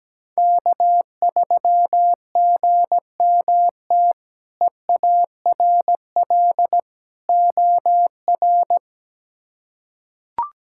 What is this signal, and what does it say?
Morse code, heard as a keyed tone, about 17 wpm, 700 Hz. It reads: K3GMT EARL OR
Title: K1USN SST QSO Practice - 17wpm